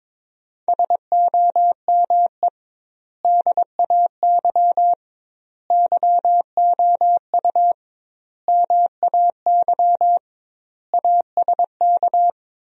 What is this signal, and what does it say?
Morse code, heard as a keyed tone, about 22 wpm, 700 Hz. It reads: SOME DAY YOU MAY ASK
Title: 2x Word-Spacing - Sentences from Top 200 Words - 22wpm